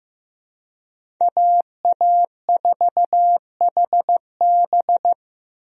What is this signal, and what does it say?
Morse code, heard as a keyed tone, about 15 wpm, 700 Hz. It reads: AA4HB
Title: USA Call Signs - 5 Characters - Repeated 2x - 15wpm